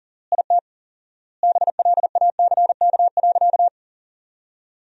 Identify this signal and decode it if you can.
Morse code, heard as a keyed tone, about 40 wpm, 700 Hz. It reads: IT BLACK.
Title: 4x Word-Spacing - Sentences from Top 400 Words - 40wpm